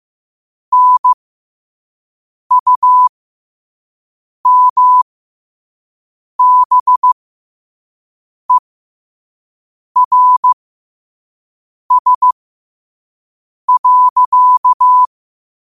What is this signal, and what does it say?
Morse code, heard as a keyed tone, about 15 wpm, spaced as for 5 wpm, 1000 Hz. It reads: NUMBERS.
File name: qotdfiles / 5wpm